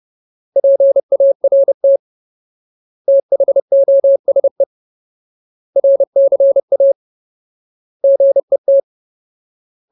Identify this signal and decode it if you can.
Morse code, heard as a keyed tone, about 30 wpm, 550 Hz. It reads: PART THOSE RCA GET
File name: Ultimate Word List / 30WPM Ultimate